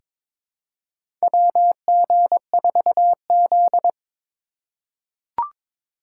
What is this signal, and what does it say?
Morse code, heard as a keyed tone, about 22 wpm, 700 Hz. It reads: WG4Z
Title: USA Call Signs - 4 Characters - 3x Character Spacing - 22wpm